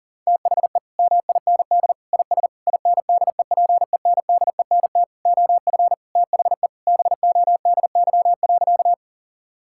Morse code, heard as a keyed tone, about 40 wpm, 700 Hz. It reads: THE MIND IS INDEPENDENT OF THE BODY.
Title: Sentences from Top 1000 Words - Review Part 6 - 40wpm